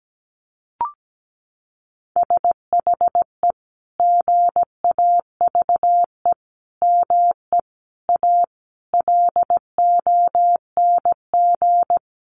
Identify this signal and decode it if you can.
Morse code, heard as a keyed tone, about 17 wpm, 700 Hz. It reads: SHE GAVE ME A LONG
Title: Sentences from Top 1400 Words 17wpm